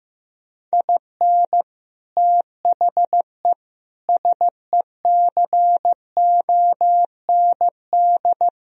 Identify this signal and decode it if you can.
Morse code, heard as a keyed tone, about 15 wpm, 700 Hz. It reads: IN THE SECOND
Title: Speed-Racing - Sets of 5 Words 15wpm